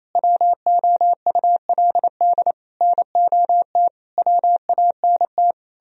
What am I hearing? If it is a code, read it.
Morse code, heard as a keyed tone, about 34 wpm, 700 Hz. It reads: WOULD NOT WANT
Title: Speed-Racing - Sets of 5 Words 28wpm